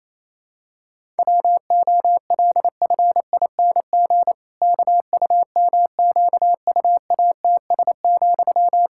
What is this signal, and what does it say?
Morse code, heard as a keyed tone, about 28 wpm, 700 Hz. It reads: WOLFING KUMQUATS,
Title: Pangram Challenge v2 - 28wpm